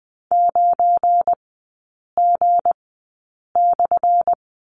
Morse code, heard as a keyed tone, about 20 wpm, spaced as for 8 wpm, 700 Hz. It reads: 9G/
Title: 178 Lesson 29e - Slash - Three Characters 20wpm